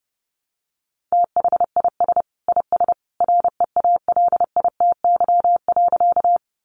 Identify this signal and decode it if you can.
Morse code, heard as a keyed tone, about 30 wpm, 700 Hz. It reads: THIS IS REALITY.
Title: Sentences from Top 1200 Words 30wpm